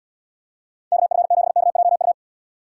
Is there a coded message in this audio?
Yes: Morse code from 0.9 to 2.1 s, heard as a keyed tone, about 77 wpm, 700 Hz. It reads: KF7KJU